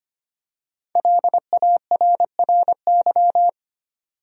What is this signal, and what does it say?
Morse code, heard as a keyed tone, about 25 wpm, 700 Hz. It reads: LARRY
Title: USA Common Names - 25wpm